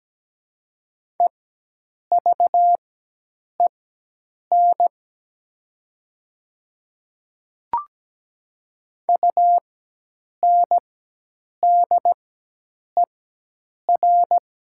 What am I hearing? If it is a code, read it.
Morse code, heard as a keyed tone, about 17 wpm, spaced as for 8 wpm, 700 Hz. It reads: EVEN UNDER
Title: 4x Character Spacing - Top 200 Words - 17wpm